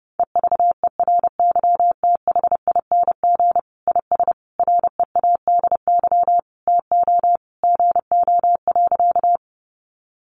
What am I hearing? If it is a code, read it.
Morse code, heard as a keyed tone, about 30 wpm, 700 Hz. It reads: EVERYTHING IS READY TO GO.
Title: Sentences from Top 700 Words 30wpm